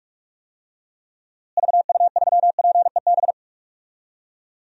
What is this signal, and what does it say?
Morse code, heard as a keyed tone, about 46 wpm, 700 Hz. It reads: VU3PEB
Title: DXCC-100 Call Signs - 6 Characters - Repeated 3x - 45wpm